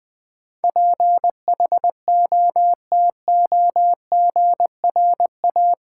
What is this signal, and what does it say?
Morse code, heard as a keyed tone, about 20 wpm, 700 Hz. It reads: PHOTOGRA
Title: Build A Word - 4 Letters At A Time - 12-Letter Words - 20wpm